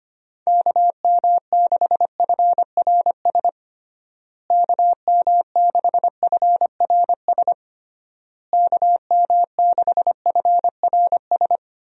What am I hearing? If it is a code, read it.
Morse code, heard as a keyed tone, about 25 wpm, 700 Hz. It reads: KM6FRS KM6FRS KM6FRS
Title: USA Call Signs - 6 Characters - Repeated 3x - 25wpm